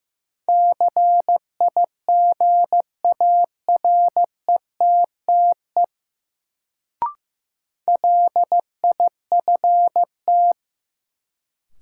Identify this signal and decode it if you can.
Morse code, heard as a keyed tone, about 16 wpm, 700 Hz. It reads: CIGARETTE LIFT
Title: Sentences from Top 1000 Words - New Word Drill - 15wpm